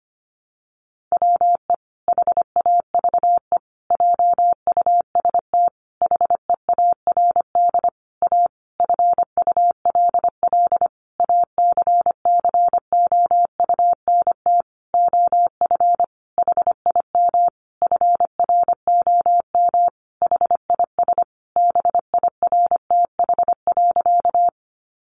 Morse code, heard as a keyed tone, about 25 wpm, 700 Hz. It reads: WE HAVE JUST HEARD A FULL ACCOUNT OF HIM FROM HIS BIRTH.
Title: Sentences from Top 900 Words 25wpm